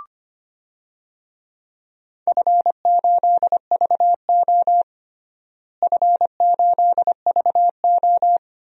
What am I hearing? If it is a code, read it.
Morse code, heard as a keyed tone, about 25 wpm, 700 Hz. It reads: F8VO F8VO
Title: DXCC-100 Call Signs - 4 Characters - Repeated 3x - 25wpm